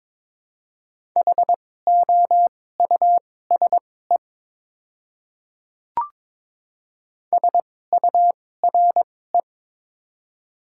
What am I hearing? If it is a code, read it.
Morse code, heard as a keyed tone, about 22 wpm, spaced as for 16 wpm, 700 Hz. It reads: HOUSE SURE
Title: 2x Character Spacing - Top 500 Words - 22wpm